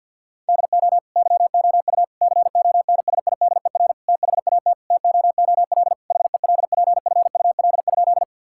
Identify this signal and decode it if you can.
Morse code, heard as a keyed tone, about 50 wpm, 700 Hz. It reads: DO YOU CONSIDER THAT TOOL HELPFUL?
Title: Sentences from Top 1000 Words - Review Part 6 - 50wpm